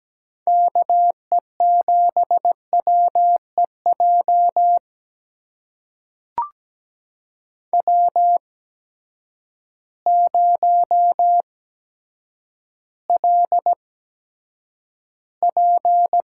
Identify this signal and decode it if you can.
Morse code, heard as a keyed tone, about 17 wpm, spaced as for 8 wpm, 700 Hz. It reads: KE7WEJ W0LP
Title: USA Call Signs - 8x Character Spacing - 17wpm